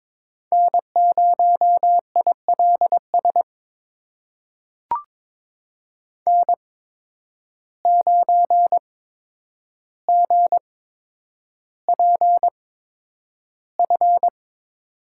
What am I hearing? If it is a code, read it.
Morse code, heard as a keyed tone, about 22 wpm, spaced as for 9 wpm, 700 Hz. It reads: N0ILS N9GPF
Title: USA Call Signs - 5 Characters - 8x Character Spacing - 22wpm